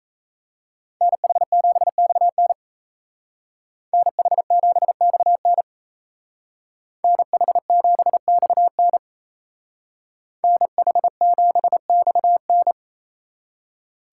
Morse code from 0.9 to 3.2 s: NH7XN

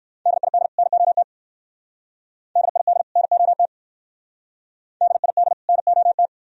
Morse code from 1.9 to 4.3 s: DID NOT

